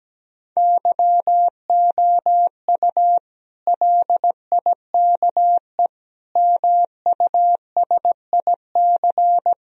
Morse code, heard as a keyed tone, about 17 wpm, 700 Hz. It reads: YOU LIKE MUSIC
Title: Sentences from Top 1000 Words - Review Part 1 - 17wpm